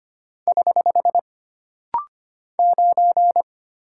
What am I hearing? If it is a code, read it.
Morse code, heard as a keyed tone, about 25 wpm, 700 Hz. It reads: <HH> 9
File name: Single Character (Letter:Number:Punctuation:Prosign) - Rapid-Fire - 25wpm